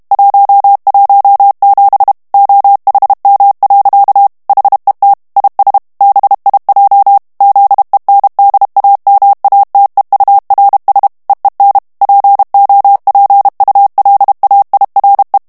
Morse code, heard as a keyed tone, about 32 wpm, 800 Hz. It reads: 117 OHM. HET IS BIJ ZENDAMATEURS EEN POPULAIRE
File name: morse mp3 / HOLLANDAIS texte clair 1 vitesse 30 mots minute